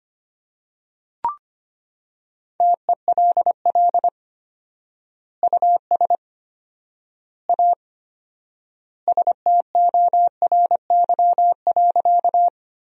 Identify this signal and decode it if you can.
Morse code, heard as a keyed tone, about 25 wpm, 700 Hz. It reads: TELL US A STORY.
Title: 4x Word-Spacing - Sentences from Top 300 Words - 25wpm